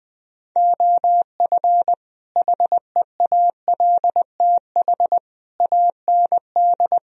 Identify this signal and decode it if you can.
Morse code, heard as a keyed tone, about 20 wpm, 700 Hz. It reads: OF HEALTH AND
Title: Speed-Racing - Sets of 5 Words 20wpm